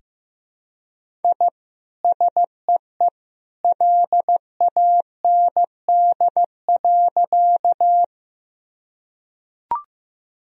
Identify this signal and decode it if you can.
Morse code, heard as a keyed tone, about 15 wpm, 700 Hz. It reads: I SEE LAND.
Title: Sentences from Top 1000 Words - Review Part 1 - 15wpm